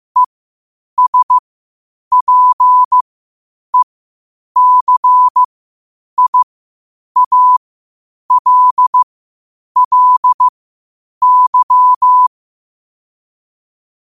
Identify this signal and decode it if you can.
Morse code, heard as a keyed tone, about 15 wpm, spaced as for 8 wpm, 1000 Hz. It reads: ESPECIALLY
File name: qotdfiles / 8wpm